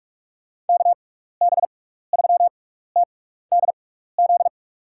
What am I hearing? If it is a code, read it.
Morse code, heard as a keyed tone, about 46 wpm, spaced as for 15 wpm, 700 Hz. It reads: KB3TDZ
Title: USA Call Signs - 6 Characters - 6x Character Spacing - 45wpm